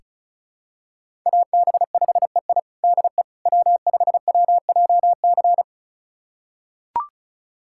Morse code, heard as a keyed tone, about 36 wpm, 700 Hz. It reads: AB5EI DE W5WJC
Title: USA Callsign DE Callsign - 5 Characters - 35wpm